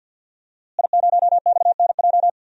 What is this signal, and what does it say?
Morse code, heard as a keyed tone, about 48 wpm, 700 Hz. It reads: I0XNJ
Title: DXCC-100 Call Signs - 5 Characters - Repeated 2x - 50wpm